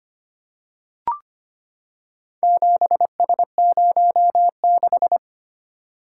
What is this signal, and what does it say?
Morse code, heard as a keyed tone, about 25 wpm, 700 Hz. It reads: 7S06
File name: practice / 25wpm / Four Letter-Number Combinations - 25wpm